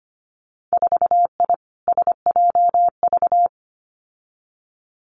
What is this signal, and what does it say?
Morse code, heard as a keyed tone, about 25 wpm, 700 Hz. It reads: 4I SJV
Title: Winter Field Day Exchanges Encore 25wpm